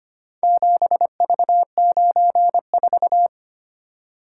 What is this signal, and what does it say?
Morse code, heard as a keyed tone, about 25 wpm, 700 Hz. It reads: 7V94
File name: practice / 25wpm / Four Letter-Number Combinations - 25wpm